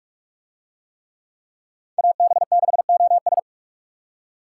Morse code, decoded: AB6OS

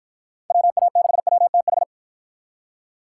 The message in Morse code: WA6WTH

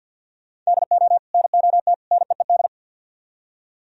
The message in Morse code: DO NOT NEED

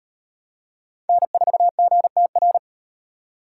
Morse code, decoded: N4GTR

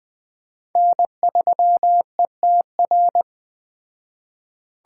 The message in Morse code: N3ETR

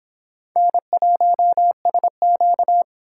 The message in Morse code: N1SQ

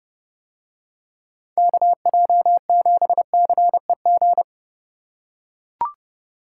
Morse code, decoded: KJ7CEG

